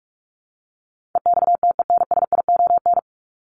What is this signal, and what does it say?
Morse code, heard as a keyed tone, about 44 wpm, 700 Hz. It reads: EXTENSION
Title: Top 1501-1750 Words 45wpm